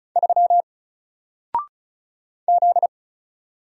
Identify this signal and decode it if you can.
Morse code, heard as a keyed tone, about 34 wpm, 700 Hz. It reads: 3 Z
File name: Single-Letter-Number-35wpm